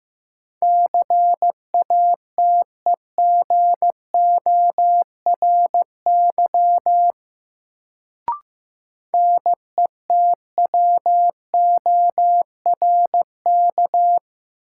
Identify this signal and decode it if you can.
Morse code, heard as a keyed tone, about 15 wpm, 700 Hz. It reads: CATEGORY NETWORK
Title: Sentences from Top 1000 Words - Review Part 7 - New Word Drill - 15wpm